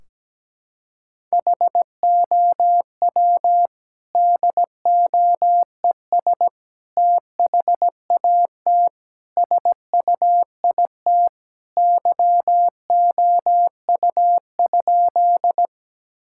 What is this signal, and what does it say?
Morse code, heard as a keyed tone, about 17 wpm, 700 Hz. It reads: HOW DOES THAT SUIT YOU?
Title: Sentences from Top 1600 Words 17wpm